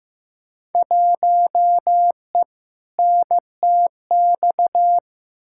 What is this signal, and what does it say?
Morse code, heard as a keyed tone, about 16 wpm, 700 Hz. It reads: 1E NTX
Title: Field Day Exchanges - 15wpm